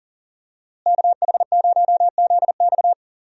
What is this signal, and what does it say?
Morse code, heard as a keyed tone, about 40 wpm, 700 Hz. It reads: KH0ZX
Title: USA Call Signs - 5 Characters - Repeated 3x - 40wpm